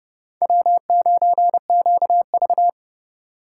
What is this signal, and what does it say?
Morse code, heard as a keyed tone, about 29 wpm, 700 Hz. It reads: W9QV